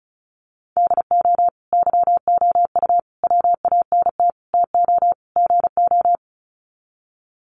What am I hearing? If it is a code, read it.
Morse code, heard as a keyed tone, about 34 wpm, 700 Hz. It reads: DO YOU WANT TO GO
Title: Sets of 5 Words Encore2 35wpm